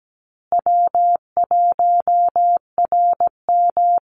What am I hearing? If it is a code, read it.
Morse code, heard as a keyed tone, about 17 wpm, 700 Hz. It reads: W1RM
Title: Easy Call Signs Repeated 3x - 17wpm